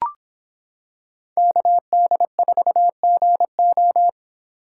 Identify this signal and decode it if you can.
Morse code, heard as a keyed tone, about 25 wpm, 700 Hz. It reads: KD4GO